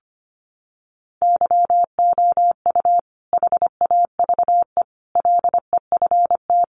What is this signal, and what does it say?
Morse code, heard as a keyed tone, about 25 wpm, 700 Hz. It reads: YOU HAVE LEFT
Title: Sentences from Top 600 Words 25wpm